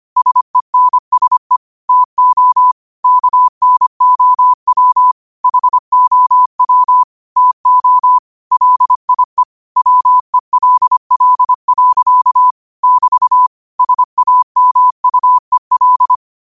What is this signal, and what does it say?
Morse code, heard as a keyed tone, about 25 wpm, 1000 Hz. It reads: SENSE TO KNOW HOW TO LIE WELL. <BT> SAMUEL
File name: qotdfiles / 25wpm